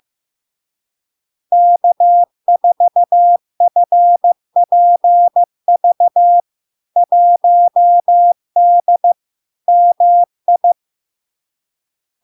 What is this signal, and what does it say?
Morse code, heard as a keyed tone, about 15 wpm, 700 Hz. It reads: K4FPV 1D MI
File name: LARK-FD-2022-Log-in-CW-15wpm-700-Hz